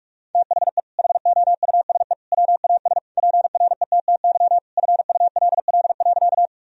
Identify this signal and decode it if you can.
Morse code, heard as a keyed tone, about 46 wpm, 700 Hz. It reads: THE HOUSE WAS PRETTY FULL.
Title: Sentences from Top 1000 Words - Review Part 3 - 45wpm